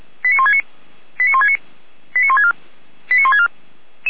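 Five-tone selective call (ZVEI), heard as quick runs of five tones, tones of about 70 ms. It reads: 89169 89169 89164 89164
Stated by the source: ZVEI five-tone call 89376 received by SDR.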